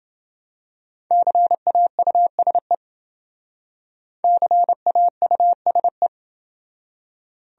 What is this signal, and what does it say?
Morse code, heard as a keyed tone, about 28 wpm, 700 Hz. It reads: CAUSE CAUSE